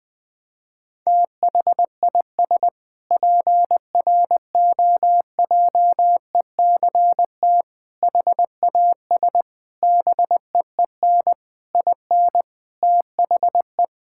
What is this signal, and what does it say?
Morse code, heard as a keyed tone, about 20 wpm, 700 Hz. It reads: THIS PROJECT HAS BEEN IN THE
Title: Sentences with 6-Word Phrases - 20wpm